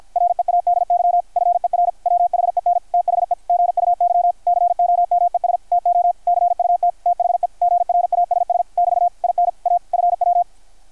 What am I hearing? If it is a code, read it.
Morse code, heard as a keyed tone, about 53 wpm, 720 Hz. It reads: READY FER WHEN THE GUY COMES TO CUT THE GRASS <BT> IN A FW
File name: archived / CRSnet